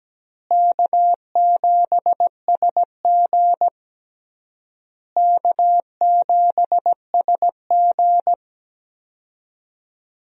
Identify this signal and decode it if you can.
Morse code, heard as a keyed tone, about 17 wpm, 700 Hz. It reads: K7SG K7SG
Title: USA Call Signs - 4 Characters - Repeated 3x - 17wpm